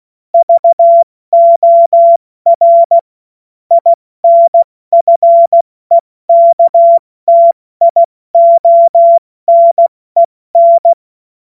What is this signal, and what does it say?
Morse code, heard as a keyed tone, about 16 wpm, spaced as for 14 wpm, 680 Hz. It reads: VOR INFEKTIONEN
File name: Morsekurs / Witze / Witz 04. Mai 2021, 70 F80